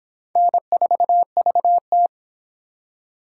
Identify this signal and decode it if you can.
Morse code, heard as a keyed tone, about 25 wpm, 700 Hz. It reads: N4VT